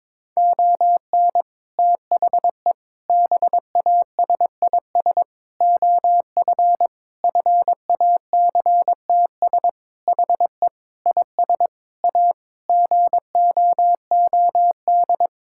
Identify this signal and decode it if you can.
Morse code, heard as a keyed tone, about 21 wpm, 700 Hz. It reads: ON THE BASIS OF FACTS HE IS A GOOD
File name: practice / 22wpm / Sentences with 3-Word Phrases - 22wpm